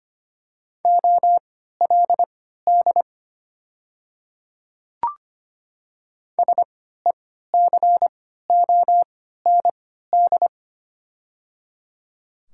Morse code, heard as a keyed tone, about 25 wpm, spaced as for 14 wpm, 700 Hz. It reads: OLD SECOND
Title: 3x Character Spacing - Top 300 Words - 25wpm